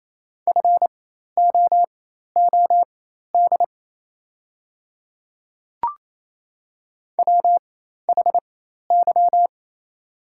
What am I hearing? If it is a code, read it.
Morse code, heard as a keyed tone, about 27 wpm, spaced as for 13 wpm, 700 Hz. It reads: FOOD WHY
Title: 4x Character Spacing - Top 300 Words - 28wpm